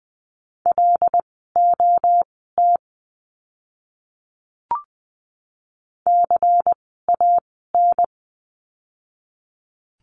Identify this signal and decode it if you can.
Morse code, heard as a keyed tone, about 20 wpm, spaced as for 15 wpm, 700 Hz. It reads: LOT CAN
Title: 115 Lesson 19f - Letter w - Words 20wpm